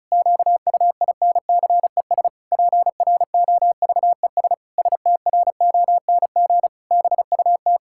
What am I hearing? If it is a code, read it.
Morse code, heard as a keyed tone, about 36 wpm, 700 Hz. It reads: QUINCES PROVES STRONG BUT
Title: Pangram Challenge v2 - 35wpm